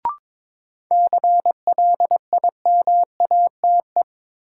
Morse code, heard as a keyed tone, about 21 wpm, 700 Hz. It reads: CLIMATE
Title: Sentences from Top 1600 Words - New Word Drill - 22wpm